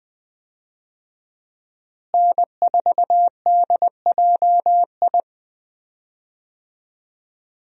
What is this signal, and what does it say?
Morse code, heard as a keyed tone, about 20 wpm, 700 Hz. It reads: N4DJI
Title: USA Call Signs - 5 Characters - Repeated 3x - 20wpm